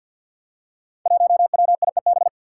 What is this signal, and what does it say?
Morse code, heard as a keyed tone, about 51 wpm, 700 Hz. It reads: 1WIEB
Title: Five Challenging Letter-Number Combinations 50wpm